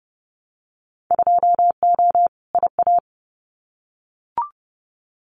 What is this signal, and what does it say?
Morse code, heard as a keyed tone, about 30 wpm, 700 Hz. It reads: 2O IA
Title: Winter Field Day Exchanges 30wpm